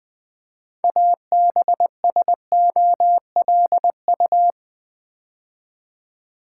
Morse code, heard as a keyed tone, about 20 wpm, 700 Hz. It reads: ABSOLU
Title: Build A Word - 2 Letters At A Time - 8-Letter Words - 20wpm